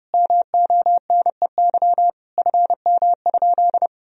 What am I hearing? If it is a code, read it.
Morse code, heard as a keyed tone, about 30 wpm, 700 Hz. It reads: MONEY FM?
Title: Sentences from Top 400 Words and Abbreviations 30wpm